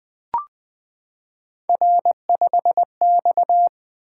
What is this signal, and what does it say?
Morse code, heard as a keyed tone, about 20 wpm, 700 Hz. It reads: R5X
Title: Three Letter-Number Combinations - 20wpm